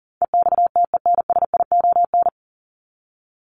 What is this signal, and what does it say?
Morse code, heard as a keyed tone, about 40 wpm, 700 Hz. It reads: EXTENSION
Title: Top 1501-1750 Words 40wpm